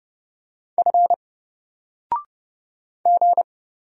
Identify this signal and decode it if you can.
Morse code, heard as a keyed tone, about 30 wpm, 700 Hz. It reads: F G
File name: Single-Letter-Number-30wpm